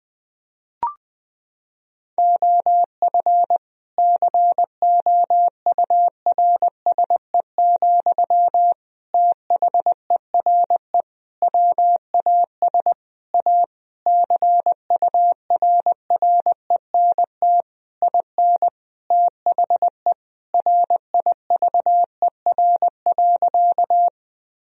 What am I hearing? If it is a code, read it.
Morse code, heard as a keyed tone, about 20 wpm, 700 Hz. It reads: OF COURSE, THERE WAS A CURRENT IN THE RIVER.
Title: Sentences from Top 1000 Words - Review Part 4 - 20wpm